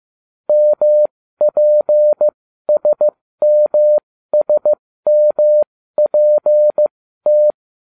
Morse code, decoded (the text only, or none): MPSMSMPT